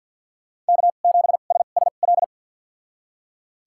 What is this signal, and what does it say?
Morse code, heard as a keyed tone, about 48 wpm, spaced as for 35 wpm, 690 Hz. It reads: K7SSL
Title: USA Call Signs - 2x Character Spacing - 50wpm